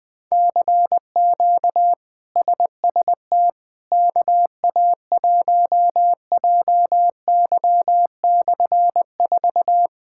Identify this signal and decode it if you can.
Morse code, heard as a keyed tone, about 20 wpm, 700 Hz. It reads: CQ SST KA1JY/4
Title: K1USN SST QSO Practice - 20wpm